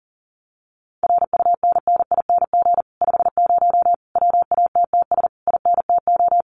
Morse code, heard as a keyed tone, about 40 wpm, 700 Hz. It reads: RUNNING 50 WATTS INTO
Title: Common QSO Phrases Encore 40wpm